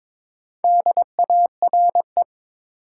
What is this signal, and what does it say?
Morse code, heard as a keyed tone, about 22 wpm, 700 Hz. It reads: DARE
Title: Four Letter Words - 22wpm